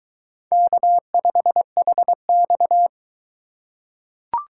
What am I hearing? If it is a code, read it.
Morse code, heard as a keyed tone, about 22 wpm, 700 Hz. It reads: K5HX